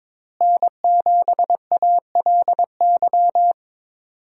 Morse code, heard as a keyed tone, about 22 wpm, 700 Hz. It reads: N7ALY
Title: USA Call Signs - 5 Characters - Repeated 2x - 22wpm